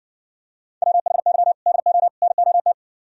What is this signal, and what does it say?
Morse code, heard as a keyed tone, about 58 wpm, 700 Hz. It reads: WHY DO NOT